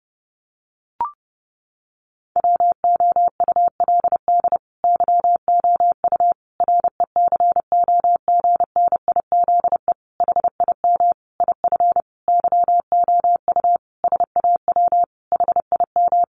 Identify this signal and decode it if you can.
Morse code, heard as a keyed tone, about 30 wpm, 700 Hz. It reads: WOULD YOU RECOGNIZE HIM IF YOU SAW HIM
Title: Sentences from Top 1400 Words 30wpm